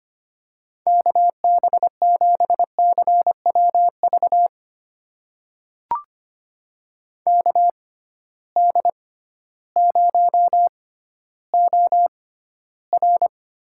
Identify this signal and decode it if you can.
Morse code, heard as a keyed tone, about 25 wpm, spaced as for 13 wpm, 700 Hz. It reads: KB7CWV KD0OR
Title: USA Call Signs - 6 Characters - 6x Character Spacing - 25wpm